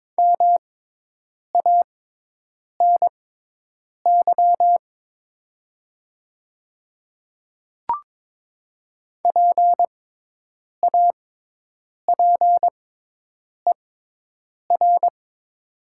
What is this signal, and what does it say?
Morse code, heard as a keyed tone, about 23 wpm, spaced as for 8 wpm, 700 Hz. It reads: MANY PAPER
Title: 6x Character Spacing - Top 300 Words - 22wpm